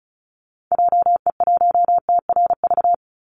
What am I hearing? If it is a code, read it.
Morse code, heard as a keyed tone, about 34 wpm, 700 Hz. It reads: JE1TRV
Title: Call Signs Repeated 3x - 35wpm